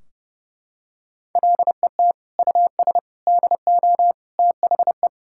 Morse code, heard as a keyed tone, about 32 wpm, 700 Hz. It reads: LET US DO THE
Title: Sentences from Top 1600 Words 30wpm